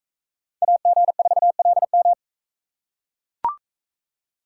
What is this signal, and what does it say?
Morse code, heard as a keyed tone, about 40 wpm, 700 Hz. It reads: AG4LM